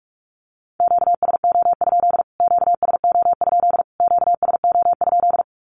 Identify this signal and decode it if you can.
Morse code, heard as a keyed tone, about 44 wpm, 700 Hz. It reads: QSO? QSO? QSO?
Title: Q-Codes Repeated 3x 45wpm